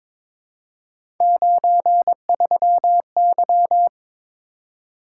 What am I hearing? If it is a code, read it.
Morse code, heard as a keyed tone, about 22 wpm, 700 Hz. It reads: 93Y
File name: practice / 22wpm / Three Letter-Number Combinations - 22wpm